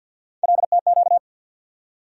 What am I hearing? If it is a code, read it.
Morse code, heard as a keyed tone, about 52 wpm, 700 Hz. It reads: LTQ
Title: USA Call Sign Postfixes - 3 Letters 50wpm